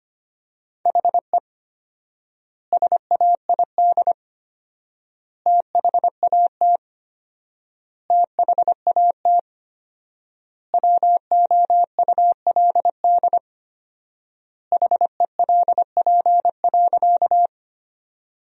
Morse code, heard as a keyed tone, about 25 wpm, 700 Hz. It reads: HE SAID THAT THAT WOULD HELP.
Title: 4x Word-Spacing - Sentences from Top 200 Words - 25wpm